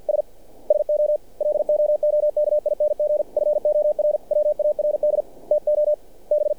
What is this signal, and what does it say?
Morse code, heard as a keyed tone, about 48 wpm, 600 Hz. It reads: S SO LOOKING FORWARD TO B
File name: archived / CRSnet